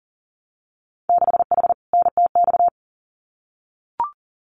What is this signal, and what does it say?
Morse code, heard as a keyed tone, about 38 wpm, 700 Hz. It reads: 6H NTX
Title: Winter Field Day Exchanges Encore 40wpm